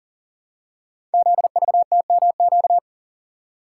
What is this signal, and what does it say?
Morse code, heard as a keyed tone, about 42 wpm, 690 Hz. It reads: ZVTMQ
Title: Code Groups - 5 Letters - 40wpm